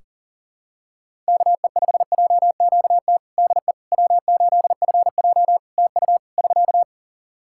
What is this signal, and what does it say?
Morse code, heard as a keyed tone, about 40 wpm, 700 Hz. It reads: KE5JQT DE W8FJ TU <SK>
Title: Practice QSO - Exchange 4 - Ending - 40wpm